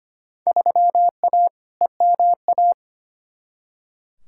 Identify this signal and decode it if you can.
Morse code, heard as a keyed tone, about 25 wpm, 700 Hz. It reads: 3A EMA
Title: Field Day Exchanges - 25wpm